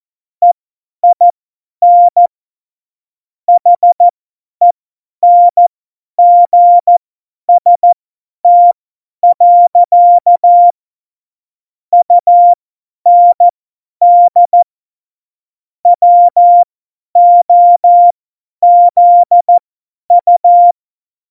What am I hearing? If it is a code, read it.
Morse code, heard as a keyed tone, about 14 wpm, spaced as for 10 wpm, 700 Hz. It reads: EIN HENGST. UND WOZU
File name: Morsekurs / Witze / Witz 15. Dezember 2020, 50 F70